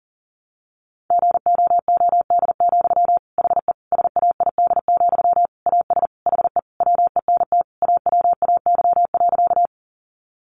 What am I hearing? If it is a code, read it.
Morse code, heard as a keyed tone, about 40 wpm, 700 Hz. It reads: GOOD, HE SAID, AS HE WENT AWAY.
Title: Sentences from Top 600 Words 40wpm